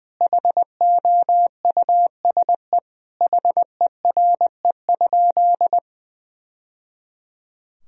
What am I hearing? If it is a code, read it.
Morse code, heard as a keyed tone, about 20 wpm, 700 Hz. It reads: HOUSE HERE?
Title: Sentences from Top 1000 Words - Review Part 1 - 20wpm